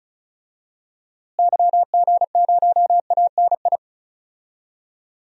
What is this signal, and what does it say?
Morse code, heard as a keyed tone, about 34 wpm, 700 Hz. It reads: YG0ANI
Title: DXCC-100 Call Signs - 6 Characters - Repeated 3x - 35wpm